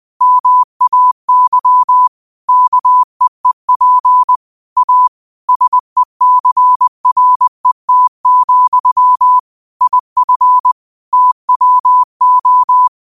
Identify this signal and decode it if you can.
Morse code, heard as a keyed tone, about 20 wpm, 1000 Hz. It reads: MAY KEEP A SECRET, IF TWO